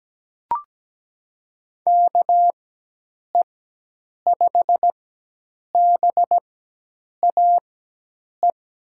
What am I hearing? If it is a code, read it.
Morse code, heard as a keyed tone, about 17 wpm, spaced as for 8 wpm, 700 Hz. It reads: KE5BAE
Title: USA Call Signs - 4x Character Spacing - 17wpm